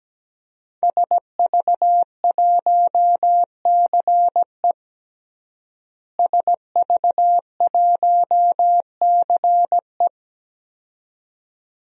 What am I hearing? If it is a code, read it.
Morse code, heard as a keyed tone, about 17 wpm, 700 Hz. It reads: SV1CE SV1CE
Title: DXCC-100 Call Signs - 5 Characters - Repeated 2x - 17wpm